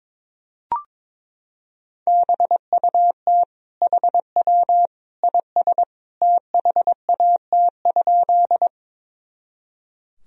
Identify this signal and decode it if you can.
Morse code, heard as a keyed tone, about 22 wpm, 700 Hz. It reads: BUT HW IS THAT?
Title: Sentences from Top 100 Words and Abbreviations 22wpm